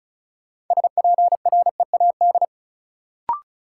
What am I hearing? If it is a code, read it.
Morse code, heard as a keyed tone, about 34 wpm, 700 Hz. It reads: SPREAD
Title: Sentences from Top 1200 Words - New Word Drill Alt - 35wpm